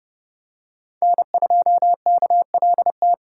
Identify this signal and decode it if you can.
Morse code, heard as a keyed tone, about 28 wpm, 700 Hz. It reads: N2KLT